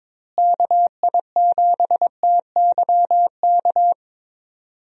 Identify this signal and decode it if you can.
Morse code, heard as a keyed tone, about 22 wpm, 700 Hz. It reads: KI7TYK
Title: USA Call Signs - 6 Characters - Repeated 2x - 22wpm